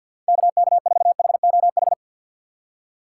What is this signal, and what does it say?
Morse code, heard as a keyed tone, about 48 wpm, 700 Hz. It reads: KK4HOH
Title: USA Call Signs - 6 Characters - 6x Character Spacing - 50wpm